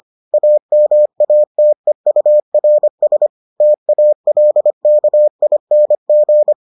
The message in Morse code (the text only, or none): AMATEURS TALKING